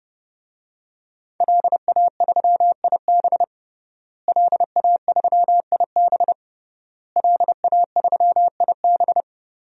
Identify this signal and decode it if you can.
Morse code, heard as a keyed tone, about 31 wpm, 700 Hz. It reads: LA3IB LA3IB LA3IB
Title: DXCC-100 Call Signs - 5 Characters - Repeated 3x - 30wpm